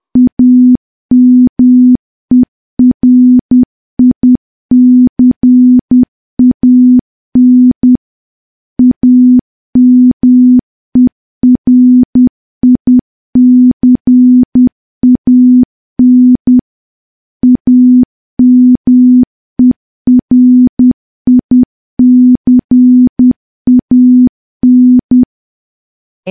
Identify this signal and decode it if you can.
Morse code, heard as a keyed tone, about 10 wpm, 250 Hz. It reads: AMERICAN AMERICAN AMERICAN
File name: morse / morse-training-10WPM